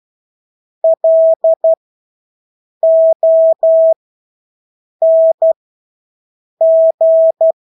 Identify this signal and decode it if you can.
Morse code, heard as a keyed tone, about 12 wpm, spaced as for 6 wpm, 650 Hz. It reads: LONG
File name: Engelske ord 12-6 wpm